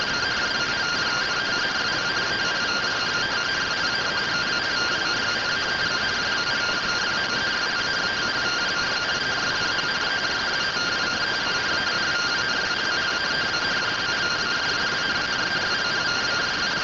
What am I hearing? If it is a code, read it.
Two-tone teleprinter signal, about 170 Hz shift.